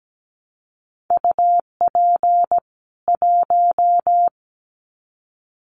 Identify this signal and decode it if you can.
Morse code, heard as a keyed tone, about 17 wpm, 700 Hz. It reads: UP 1
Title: Common QSO Phrases 17wpm